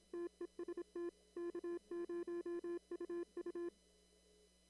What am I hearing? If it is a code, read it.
Morse code, heard as a keyed tone, about 26 wpm, 340 Hz. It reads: TEST K0UU
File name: CW / kt5j wpx cw-op-k5pi